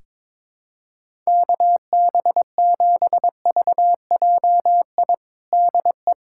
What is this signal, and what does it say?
Morse code, heard as a keyed tone, about 22 wpm, 700 Hz. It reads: KB7VJI DE
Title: USA Callsign DE Callsign - 6 Characters - 22wpm